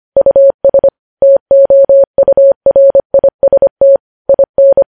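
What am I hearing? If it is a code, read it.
Morse code, heard as a keyed tone, about 25 wpm, 550 Hz. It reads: US TOURIST IN
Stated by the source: News headlines sent in Morse at 25.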